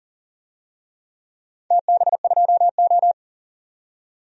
Morse code, heard as a keyed tone, about 38 wpm, 700 Hz. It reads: TB2O
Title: DXCC-100 Call Signs - 4 Characters - 40wpm